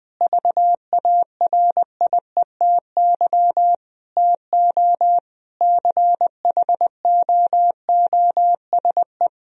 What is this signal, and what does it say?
Morse code, heard as a keyed tone, about 20 wpm, 700 Hz. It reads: VARIETY TO CHOOSE
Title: Sentences from Top 1000 Words - Review Part 4 - 20wpm